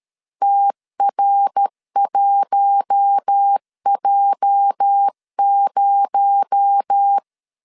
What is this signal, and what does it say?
Morse code, heard as a keyed tone, about 13 wpm, 800 Hz. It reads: TR1J0
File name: LEZIONI / 8 YJ45 / 8 YJ 12wpm 2235